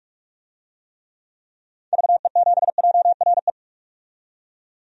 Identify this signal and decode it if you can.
Morse code, heard as a keyed tone, about 46 wpm, 700 Hz. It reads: VE7JRE
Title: DXCC-100 Call Signs - 6 Characters - 45wpm